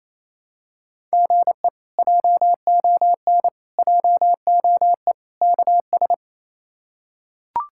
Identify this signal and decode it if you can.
Morse code, heard as a keyed tone, about 28 wpm, 700 Hz. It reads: GE JON JOE KS
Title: K1USN SST QSO Practice - 28wpm